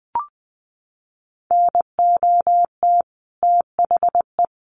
Morse code, heard as a keyed tone, about 20 wpm, 700 Hz. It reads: NOT THE
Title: Sentences from Top 900 Words 20wpm